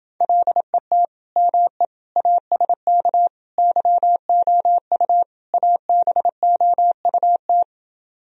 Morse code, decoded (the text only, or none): LET ME ASK YOU ABOUT